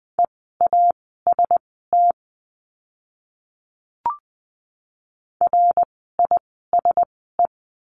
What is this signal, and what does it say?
Morse code, heard as a keyed tone, about 20 wpm, spaced as for 15 wpm, 700 Hz. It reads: EAST RISE
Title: 054 Lesson 10f - Letter r - Words 20wpm